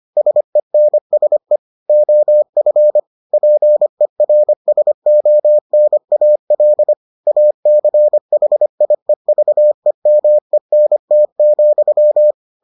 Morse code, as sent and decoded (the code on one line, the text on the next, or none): ... -. ... / --- ..-. / .--. .-. ... --- -. .- .-.. / .- -.-. .... .. ...- . -- . -. - --..--
SENSE OF PERSONAL ACHIEVEMENT,